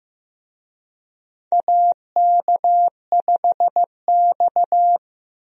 Morse code, heard as a keyed tone, about 15 wpm, 700 Hz. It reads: AK5X
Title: USA Call Signs - 4 Characters - Repeated 2x - 15wpm